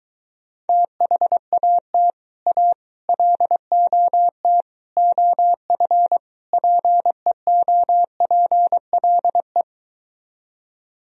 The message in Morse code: THAT A LOT OF PEOPLE

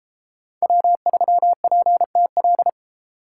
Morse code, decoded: W3PTL

